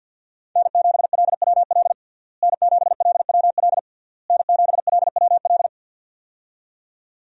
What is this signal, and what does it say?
Morse code, heard as a keyed tone, about 52 wpm, 700 Hz. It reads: N7LWL N7LWL N7LWL
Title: USA Call Signs - 5 Characters - Repeated 3x - 50wpm